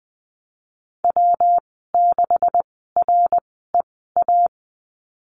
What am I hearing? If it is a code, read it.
Morse code, heard as a keyed tone, about 20 wpm, spaced as for 15 wpm, 700 Hz. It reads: W6REA
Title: 128 Lesson 21g - Number 6 - Call Signs 20wpm